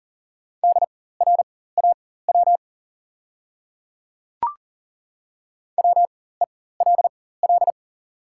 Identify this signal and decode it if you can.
Morse code, heard as a keyed tone, about 40 wpm, spaced as for 19 wpm, 700 Hz. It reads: DRAW WELL
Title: 4x Character Spacing - Top 500 Words - 40wpm